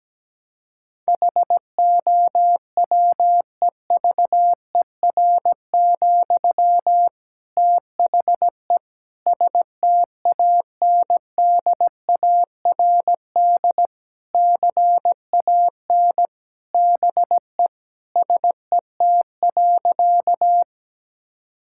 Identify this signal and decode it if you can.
Morse code, heard as a keyed tone, about 17 wpm, 700 Hz. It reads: HOWEVER, THE STANDARD CAN BE SET.
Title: Sentences from Top 1000 Words - Review Part 6 - 17wpm